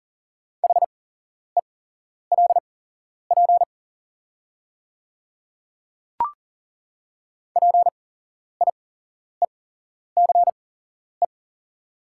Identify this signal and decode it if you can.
Morse code, heard as a keyed tone, about 40 wpm, spaced as for 11 wpm, 690 Hz. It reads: HELP PIECE
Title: 8x Character Spacing - Top 500 Words - 40wpm